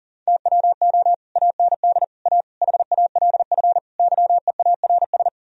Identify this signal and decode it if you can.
Morse code, heard as a keyed tone, about 50 wpm, 700 Hz. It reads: TWO AND A HALF YEARS